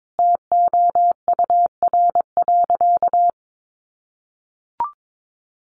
Morse code, heard as a keyed tone, about 22 wpm, 700 Hz. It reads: TOUR.
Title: Sentences from Top 1400 Words 22wpm